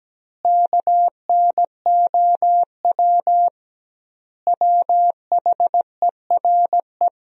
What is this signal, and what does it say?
Morse code, heard as a keyed tone, about 17 wpm, 700 Hz. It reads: KNOW WHERE
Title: 2x Word-Spacing - Sentences from Top 300 Words - 17wpm